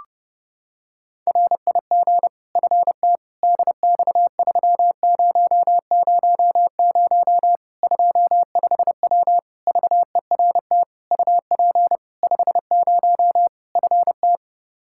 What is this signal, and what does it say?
Morse code, heard as a keyed tone, about 32 wpm, 700 Hz. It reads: RIG FT DX3000 25W VERT UP 50 FT
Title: Practice QSO - Exchange 2 - Info - 30wpm